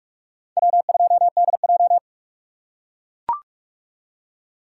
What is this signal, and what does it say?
Morse code, heard as a keyed tone, about 44 wpm, 700 Hz. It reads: W2DJ